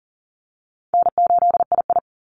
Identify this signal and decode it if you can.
Morse code, heard as a keyed tone, about 40 wpm, 700 Hz. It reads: N8II
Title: Easy Call Signs Repeated 2x - 40wpm